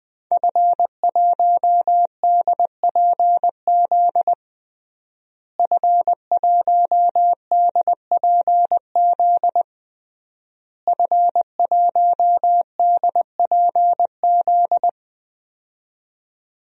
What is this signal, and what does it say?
Morse code, heard as a keyed tone, about 20 wpm, 700 Hz. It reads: F1DPZ F1DPZ F1DPZ
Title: DXCC-100 Call Signs - 5 Characters - Repeated 3x - 20wpm